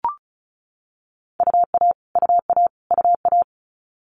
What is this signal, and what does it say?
Morse code, heard as a keyed tone, about 36 wpm, 700 Hz. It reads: UA UA UA
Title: Multi-letter Phonemes Repeated 3x 35wpm